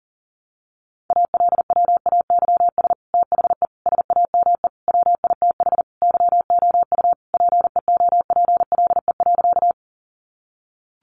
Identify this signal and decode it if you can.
Morse code, heard as a keyed tone, about 40 wpm, 700 Hz. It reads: ALWAYS THE SAME WITH YOU PEOPLE.
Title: Sentences from Top 300 Words 40wpm